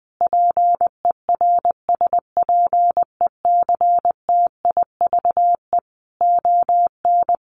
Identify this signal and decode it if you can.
Morse code, heard as a keyed tone, about 20 wpm, 700 Hz. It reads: PERSPECTIVE ON
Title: Sentences from Top 1200 Words 20wpm